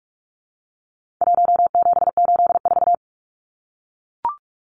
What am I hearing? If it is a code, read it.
Morse code, heard as a keyed tone, about 44 wpm, 700 Hz. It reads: 1784
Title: Numbers - 4-Digit 45wpm